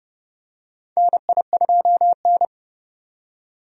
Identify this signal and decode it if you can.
Morse code, heard as a keyed tone, about 30 wpm, 700 Hz. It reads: NI2N
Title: USA Call Signs - 4 Characters - 30wpm